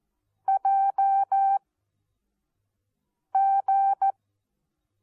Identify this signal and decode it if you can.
Morse code, heard as a keyed tone, about 14 wpm, spaced as for 4 wpm, 780 Hz. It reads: JG
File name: Onlinelehrgang / morsen / DJ4UF Morsen14